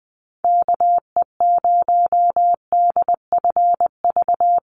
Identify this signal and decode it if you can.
Morse code, heard as a keyed tone, about 20 wpm, 700 Hz. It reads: KE0DFV
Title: USA Call Signs Repeated 3x 20wpm